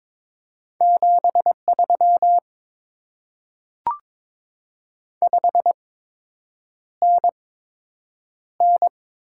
Morse code, heard as a keyed tone, about 22 wpm, spaced as for 8 wpm, 700 Hz. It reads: 73 5NN
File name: Top 20 QSO Elements - 8x Character Spacing - 22wpm